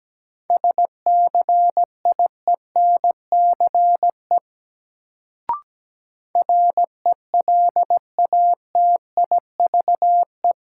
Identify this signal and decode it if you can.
Morse code, heard as a keyed tone, about 17 wpm, 700 Hz. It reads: SCIENCE RELATIVE
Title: Sentences from Top 1400 Words - New Word Drill Alt - 17wpm